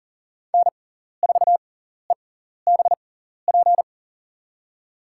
Morse code, heard as a keyed tone, about 40 wpm, spaced as for 14 wpm, 700 Hz. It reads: N4EBP
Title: USA Call Signs - 6x Character Spacing - 40wpm